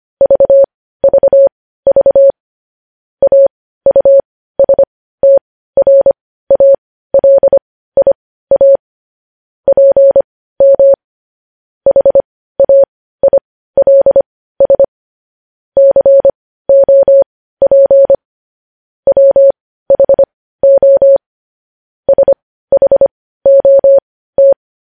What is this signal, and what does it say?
Morse code, heard as a keyed tone, about 25 wpm, spaced as for 15 wpm, 550 Hz. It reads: VVV AUSTRALIA PM HAILS COP WHO SHOT
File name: podcast / News headlines-1x-25x15